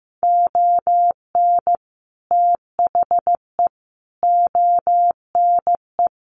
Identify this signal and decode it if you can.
Morse code, heard as a keyed tone, about 15 wpm, 700 Hz. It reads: ON THE ONE
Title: Sets of 4 Words 15wpm